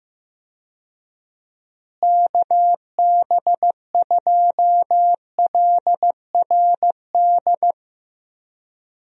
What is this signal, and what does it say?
Morse code, heard as a keyed tone, about 16 wpm, 700 Hz. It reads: KB2LRD
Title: USA Call Signs - 6 Characters - Repeated 3x - 15wpm